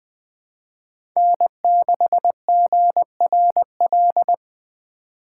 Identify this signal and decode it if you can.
Morse code, heard as a keyed tone, about 21 wpm, 700 Hz. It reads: N6GRL
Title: USA Call Signs - 4x Character Spacing - 20wpm